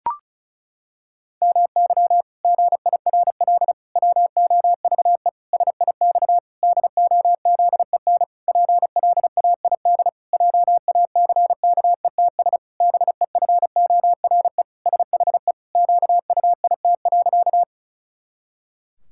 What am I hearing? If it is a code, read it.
Morse code, heard as a keyed tone, about 34 wpm, 700 Hz. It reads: MY GIRL WOVE SIX DOZEN PLAID JACKETS BEFORE SHE QUIT.
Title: Pangram Challenge v2 - 35wpm